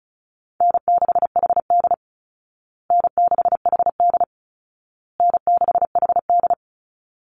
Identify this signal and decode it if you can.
Morse code, heard as a keyed tone, about 34 wpm, 700 Hz. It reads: N6HD N6HD N6HD
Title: Easy Call Signs Repeated 3x - 35wpm